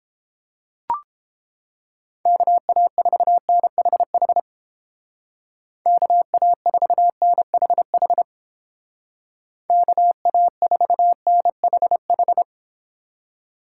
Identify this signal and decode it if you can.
Morse code, heard as a keyed tone, about 30 wpm, 700 Hz. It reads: KA4NHH KA4NHH KA4NHH